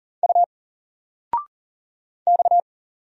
Morse code, heard as a keyed tone, about 40 wpm, 700 Hz. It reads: U X
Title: Single Letter-Number 40wpm